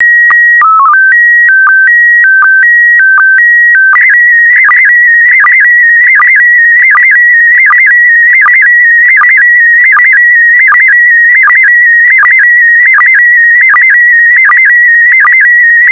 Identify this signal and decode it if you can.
SSTV (slow-scan television), PD 180.
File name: SSTV PD180 Leo